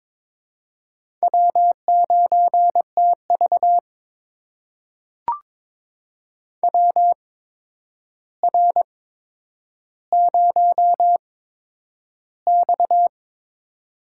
Morse code, heard as a keyed tone, about 23 wpm, spaced as for 10 wpm, 700 Hz. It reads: W9TV WR0X
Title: USA Call Signs - 4 Characters - 8x Character Spacing - 22wpm